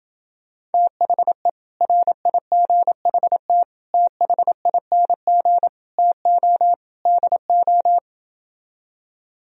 Morse code, heard as a keyed tone, about 28 wpm, 700 Hz. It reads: THE RIGHT THING TO DO